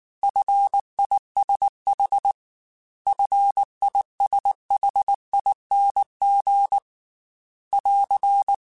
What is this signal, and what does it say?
Morse code, heard as a keyed tone, about 19 wpm, 800 Hz. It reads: FISH FISHING <AR>
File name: Fs / Single F mp3s / Single F Prefixes 3x2 Twice @18